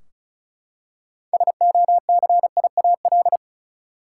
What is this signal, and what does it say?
Morse code, heard as a keyed tone, about 36 wpm, 700 Hz. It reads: SOCIAL
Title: K1USN SST QSO Practice 35wpm